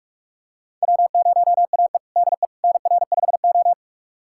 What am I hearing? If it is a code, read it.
Morse code, heard as a keyed tone, about 41 wpm, 700 Hz. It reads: W0AE DE NR5O
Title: USA Callsign DE Callsign - 4 Characters - 45wpm